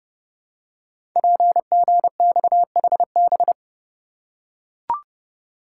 Morse code, heard as a keyed tone, about 30 wpm, 700 Hz. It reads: PGXHB
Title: Code Groups - 5 Letters - 30wpm